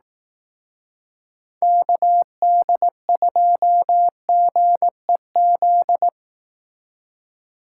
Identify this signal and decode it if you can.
Morse code, heard as a keyed tone, about 18 wpm, 700 Hz. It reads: KD2GEZ
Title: Speed Racing - USA 2x3 Call Signs 15wpm